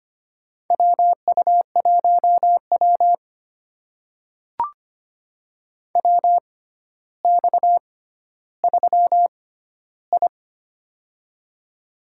Morse code, heard as a keyed tone, about 25 wpm, spaced as for 13 wpm, 700 Hz. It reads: WU1W WX3I
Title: USA Call Signs - 4 Characters - 6x Character Spacing - 25wpm